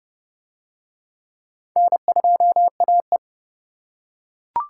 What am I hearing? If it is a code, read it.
Morse code, heard as a keyed tone, about 29 wpm, 700 Hz. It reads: N2AE